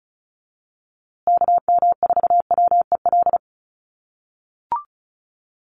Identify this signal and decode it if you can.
Morse code, heard as a keyed tone, about 34 wpm, 700 Hz. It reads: KM4WEL
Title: USA Call Signs Repeated 2x 35wpm